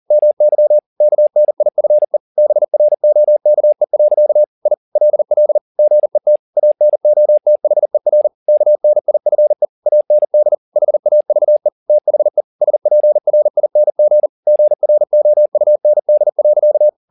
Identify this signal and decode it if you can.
Morse code, heard as a keyed tone, about 40 wpm, 600 Hz. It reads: MY KNIFE BROKE. I LL GET ANOTHER KNIFE AND HAVE THE SPRING GROUND.
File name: the old man and the sea 2 40wpm